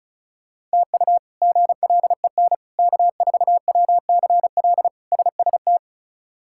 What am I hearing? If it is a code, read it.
Morse code, heard as a keyed tone, about 36 wpm, 700 Hz. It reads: TU GLEN K4WCL SST
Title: K1USN SST QSO Practice - 35wpm